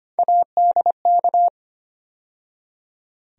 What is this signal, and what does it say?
Morse code, heard as a keyed tone, about 25 wpm, 700 Hz. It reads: ADK
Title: USA Call Sign Postfixes - 3 Letters 25wpm